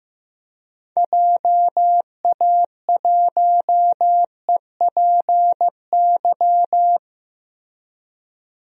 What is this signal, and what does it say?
Morse code, heard as a keyed tone, about 15 wpm, 700 Hz. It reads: JA1EPY
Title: DXCC-100 Call Signs - 6 Characters - Repeated 2x - 15wpm